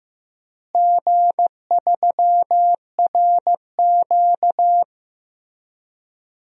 Morse code, heard as a keyed tone, about 15 wpm, 700 Hz. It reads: G3RQ
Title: DXCC-100 Call Signs - 4 Characters - Repeated 3x - 15wpm